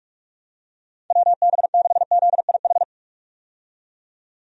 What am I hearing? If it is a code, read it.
Morse code, heard as a keyed tone, about 47 wpm, 700 Hz. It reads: WB6ZIH
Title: USA Call Signs - 6 Characters - 45wpm